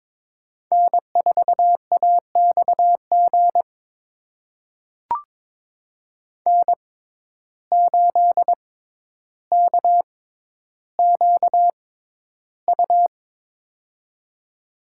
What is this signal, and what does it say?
Morse code, heard as a keyed tone, about 22 wpm, spaced as for 11 wpm, 700 Hz. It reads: N4AXG N8KQU
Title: USA Call Signs - 5 Characters - 6x Character Spacing - 22wpm